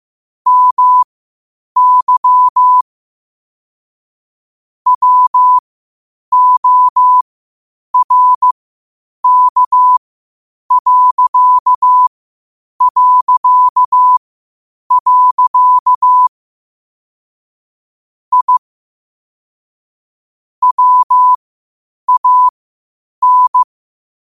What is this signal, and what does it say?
Morse code, heard as a keyed tone, about 15 wpm, spaced as for 8 wpm, 1000 Hz. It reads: MY WORK... I WAN
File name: qotdfiles / 8wpm